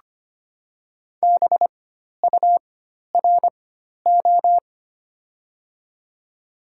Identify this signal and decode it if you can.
Morse code, heard as a keyed tone, about 25 wpm, spaced as for 12 wpm, 700 Hz. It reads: BURO
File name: Top 90 QSO Elements - 4x Character Spacing - 25wpm